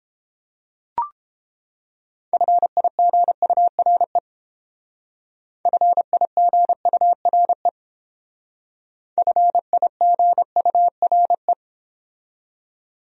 Morse code, heard as a keyed tone, about 30 wpm, 700 Hz. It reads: FIGURE FIGURE FIGURE